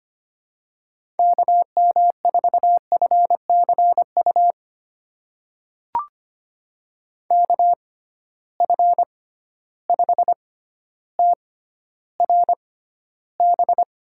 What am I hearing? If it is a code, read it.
Morse code, heard as a keyed tone, about 25 wpm, spaced as for 13 wpm, 700 Hz. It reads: KM4FCU KF5TRB
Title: USA Call Signs - 6 Characters - 6x Character Spacing - 25wpm